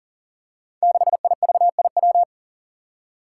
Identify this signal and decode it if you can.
Morse code, heard as a keyed tone, about 38 wpm, 700 Hz. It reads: 6IVIW
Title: Five Challenging Letter-Number Combinations 40wpm